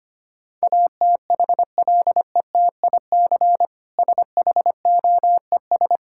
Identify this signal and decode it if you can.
Morse code, heard as a keyed tone, about 25 wpm, 700 Hz. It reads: ATHLETIC SHOES
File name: Easy Words - Speed-Racing with Hints - 25wpm